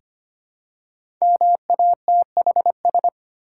Morse code, heard as a keyed tone, about 25 wpm, 700 Hz. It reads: MATHS
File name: 5 Letter Words - First 3 Letters Given - 25wpm